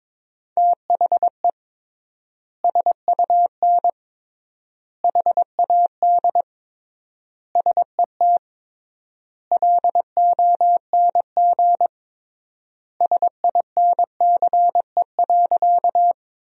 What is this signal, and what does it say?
Morse code, heard as a keyed tone, about 21 wpm, 700 Hz. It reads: THE SUN HAD SET LONG SINCE.
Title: 3x Word-Spacing - Sentences from Top 300 Words - 22wpm